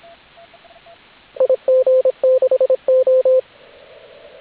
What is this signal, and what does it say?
Morse code, heard as a keyed tone, about 26 wpm, 520 Hz, with radio noise. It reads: IG6O